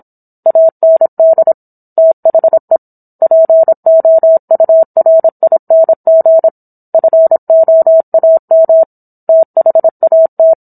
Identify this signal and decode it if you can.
Morse code, heard as a keyed tone, about 27 wpm, 650 Hz. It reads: AND THE POURING FOAM THAT
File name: Book One / 17-Chapter 17 - 26 WPM